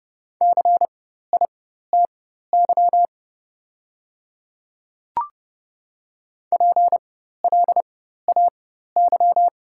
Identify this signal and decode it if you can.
Morse code, heard as a keyed tone, about 30 wpm, spaced as for 14 wpm, 700 Hz. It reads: CITY PLAY
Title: 4x Character Spacing - Top 300 Words - 30wpm